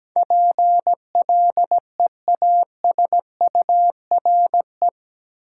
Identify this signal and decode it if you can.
Morse code, heard as a keyed tone, about 17 wpm, 700 Hz. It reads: PLEASURE
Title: Sentences from Top 1600 Words - New Word Drill Alt - 17wpm